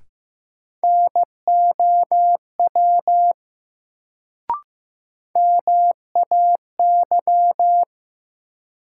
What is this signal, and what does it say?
Morse code, heard as a keyed tone, about 15 wpm, 700 Hz. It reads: NOW MAY
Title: Sets of 2 Words - New Word Drill - 15wpm